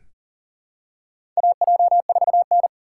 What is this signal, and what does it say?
Morse code, heard as a keyed tone, about 40 wpm, 700 Hz. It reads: AJ4N
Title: USA Call Signs - 4 Characters - 6x Character Spacing - 40wpm